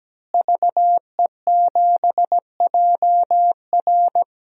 Morse code, heard as a keyed tone, about 17 wpm, 700 Hz. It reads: VE7JR
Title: DXCC-100 Call Signs - 5 Characters - Repeated 2x - 17wpm